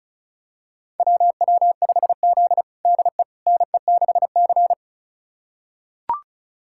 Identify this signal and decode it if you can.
Morse code, heard as a keyed tone, about 36 wpm, 700 Hz. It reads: WW5Z DE NE6C
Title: USA Callsign DE Callsign - 4 Characters - 35wpm